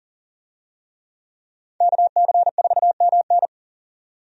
Morse code, heard as a keyed tone, about 40 wpm, 700 Hz. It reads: KC4MN
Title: USA Call Signs - 5 Characters - 40wpm